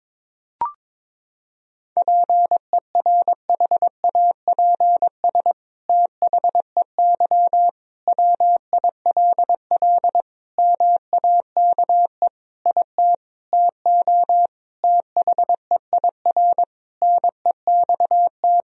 Morse code, heard as a keyed tone, about 22 wpm, 700 Hz. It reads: PERHAPS THEY WILL MAKE IT TO THEIR NEXT
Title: Sentences from Top 1000 Words - Review Part 2 - 22wpm